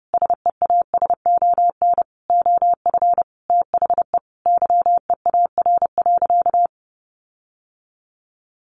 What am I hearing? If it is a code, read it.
Morse code, heard as a keyed tone, about 30 wpm, 700 Hz. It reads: SEASON OF THE YEAR.
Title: Sentences from Top 1400 Words 30wpm